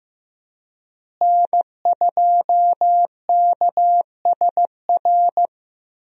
Morse code, heard as a keyed tone, about 15 wpm, 700 Hz. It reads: N2KSR
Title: USA Call Signs - 5 Characters - Repeated 3x - 15wpm